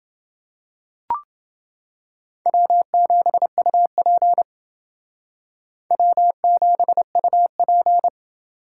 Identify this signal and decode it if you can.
Morse code, heard as a keyed tone, about 29 wpm, 700 Hz. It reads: W7UP W7UP